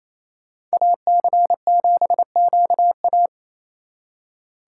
Morse code, heard as a keyed tone, about 36 wpm, 700 Hz. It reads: AC7QA